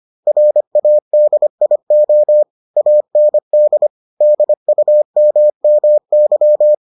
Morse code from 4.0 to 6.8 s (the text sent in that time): DUMMY